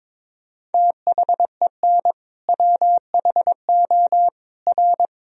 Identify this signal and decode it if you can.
Morse code, heard as a keyed tone, about 22 wpm, 700 Hz. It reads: THEN WHO R
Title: Sentences from Top 100 Words and Abbreviations 22wpm